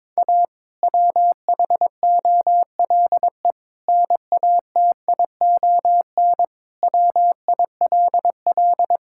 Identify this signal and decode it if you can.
Morse code, heard as a keyed tone, about 22 wpm, 700 Hz. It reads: A WHOLE NATION WILL
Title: Sentences from Top 1000 Words - Review Part 6 - 22wpm